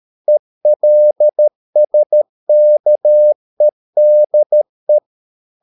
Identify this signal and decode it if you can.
Morse code, heard as a keyed tone, about 13 wpm, 600 Hz. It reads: ELSKEDE
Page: Ferie 13 wpm